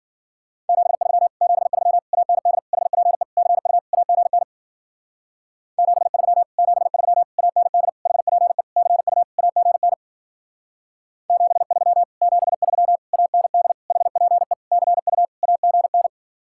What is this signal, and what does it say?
Morse code, heard as a keyed tone, about 48 wpm, 700 Hz. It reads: 73 73 AND HPE CU AGN 73 73 AND HPE CU AGN 73 73 AND HPE CU AGN
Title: Speed-Racing Common QSO Phrases - 40wpm